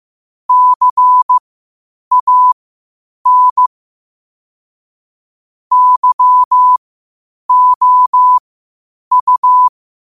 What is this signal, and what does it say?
Morse code, heard as a keyed tone, about 15 wpm, spaced as for 8 wpm, 1000 Hz. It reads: CAN YOU